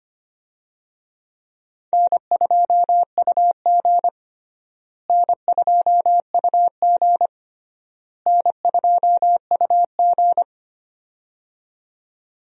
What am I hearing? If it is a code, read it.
Morse code, heard as a keyed tone, about 25 wpm, 700 Hz. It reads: N2UG N2UG N2UG
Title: USA Call Signs - 4 Characters - Repeated 3x - 25wpm